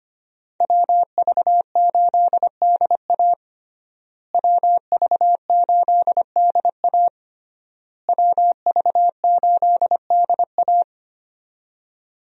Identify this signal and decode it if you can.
Morse code, heard as a keyed tone, about 25 wpm, 700 Hz. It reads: WV8DA WV8DA WV8DA
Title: USA Call Signs - 5 Characters - Repeated 3x - 25wpm